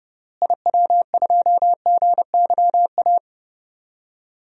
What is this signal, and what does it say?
Morse code, heard as a keyed tone, about 30 wpm, 700 Hz. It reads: IW2GYA
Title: DXCC-100 Call Signs - 6 Characters - Repeated 2x - 30wpm